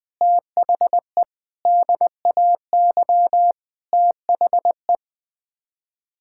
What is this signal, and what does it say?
Morse code, heard as a keyed tone, about 20 wpm, 700 Hz. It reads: THE DAY THE
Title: Sets of 8 Words - 20wpm